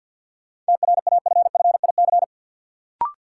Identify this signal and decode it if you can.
Morse code, heard as a keyed tone, about 52 wpm, 700 Hz. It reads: TRAFFIC
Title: Sentences from Top 1300 Words - New Word Drill Alt - 50wpm